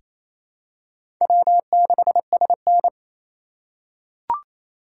Morse code, heard as a keyed tone, about 28 wpm, 700 Hz. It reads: W6SN
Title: USA Call Signs - 4 Characters - 3x Character Spacing - 28wpm